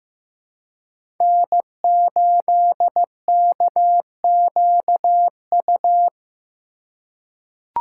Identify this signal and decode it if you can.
Morse code, heard as a keyed tone, about 15 wpm, 700 Hz. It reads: N8KQU
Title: USA Call Signs - 5 Characters - 2x Character Spacing - 15wpm